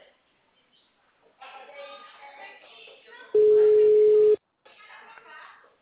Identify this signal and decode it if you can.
Ringback tone (Europe, 425 Hz), with speech.
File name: out- -2009- - - .43708